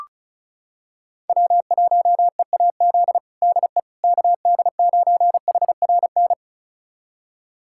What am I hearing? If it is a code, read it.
Morse code, heard as a keyed tone, about 36 wpm, 700 Hz. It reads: W1EAZ DE KD9HRN
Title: USA Callsign DE Callsign 35wpm